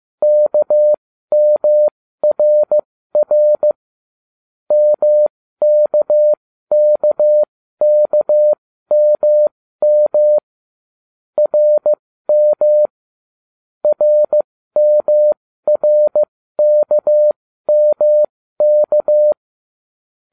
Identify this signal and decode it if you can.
Morse code, heard as a keyed tone, about 15 wpm, spaced as for 13 wpm, 600 Hz. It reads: KMRR MKKKMM RM RMRKMK